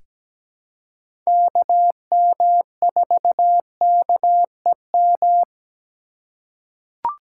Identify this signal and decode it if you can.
Morse code, heard as a keyed tone, about 17 wpm, 700 Hz. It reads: KM4KEM
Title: USA Call Signs - 3x Character Spacing - 17wpm